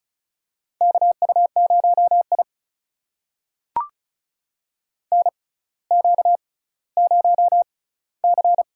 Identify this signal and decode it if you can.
Morse code, heard as a keyed tone, about 36 wpm, spaced as for 18 wpm, 700 Hz. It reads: KU0I NQ0C
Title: USA Call Signs - 4 Characters - 6x Character Spacing - 35wpm